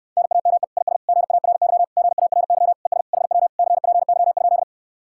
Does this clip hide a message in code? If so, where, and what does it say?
Yes: Morse code from 0.2 to 4.6 s, heard as a keyed tone, about 68 wpm, 700 Hz. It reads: NAME IS GARY GARY ES HW CPY?